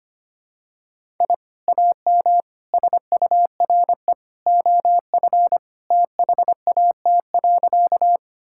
Morse code, heard as a keyed tone, about 25 wpm, 700 Hz. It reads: I AM SURE OF THAT.
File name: practice / 25wpm / Sentences with 2-Word Phrases - 25wpm